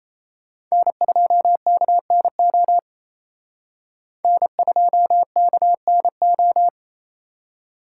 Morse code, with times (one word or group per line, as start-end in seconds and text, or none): 0.7-2.8 N2KNO
4.2-6.7 N2KNO